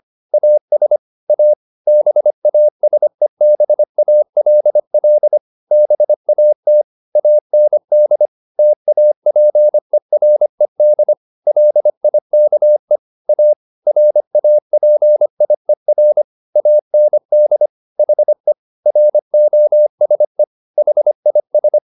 Morse code, heard as a keyed tone, about 25 wpm, 600 Hz. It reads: AS A BASEBALL BAT AND TAPERED LIKE A RAPIER AND HE ROSE HIS
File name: the old man and the sea part 1 25wpm